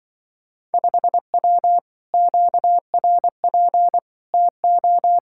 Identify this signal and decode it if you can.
Morse code, heard as a keyed tone, about 24 wpm, 700 Hz. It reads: 5W QRP TO
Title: Speed-Racing Common QSO Phrases - 20wpm